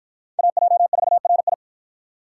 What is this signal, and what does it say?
Morse code, heard as a keyed tone, about 51 wpm, 700 Hz. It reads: AJ4RI